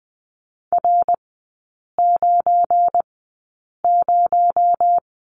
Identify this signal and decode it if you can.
Morse code, heard as a keyed tone, about 20 wpm, spaced as for 8 wpm, 700 Hz. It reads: R90
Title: 218 Lesson 35e - Number 0 - Three Characters 20wpm